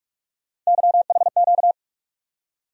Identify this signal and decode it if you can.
Morse code, heard as a keyed tone, about 42 wpm, 700 Hz. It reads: YHQ
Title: USA Call Sign Postfixes - 3 Letters 45wpm